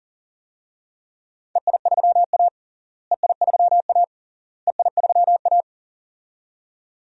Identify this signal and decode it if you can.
Morse code, heard as a keyed tone, about 40 wpm, 700 Hz. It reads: EI3A EI3A EI3A
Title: DXCC-100 Call Signs - 4 Characters - Repeated 3x - 40wpm